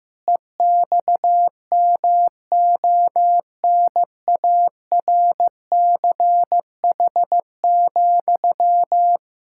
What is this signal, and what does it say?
Morse code, heard as a keyed tone, about 15 wpm, 700 Hz. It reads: EXMONARCH,
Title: Pangram Challenge v2 - 15wpm